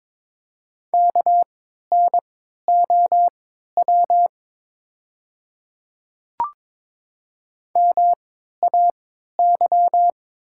Morse code, heard as a keyed tone, about 22 wpm, spaced as for 13 wpm, 700 Hz. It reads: KNOW MAY
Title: 3x Character Spacing - Top 100 Words - 22wpm